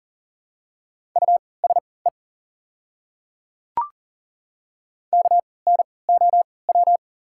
Morse code, heard as a keyed tone, about 40 wpm, spaced as for 23 wpm, 690 Hz. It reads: USE KNOW
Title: 3x Character Spacing - Top 200 Words - 40wpm